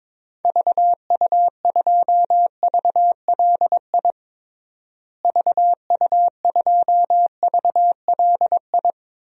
Morse code, heard as a keyed tone, about 22 wpm, 700 Hz. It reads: VU2VLI VU2VLI
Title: DXCC-100 Call Signs - 6 Characters - Repeated 2x - 22wpm